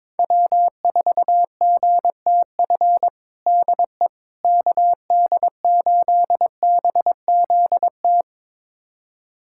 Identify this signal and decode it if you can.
Morse code, heard as a keyed tone, about 22 wpm, 700 Hz. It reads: W4GTF DE KD8BZT
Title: USA Callsign DE Callsign 22wpm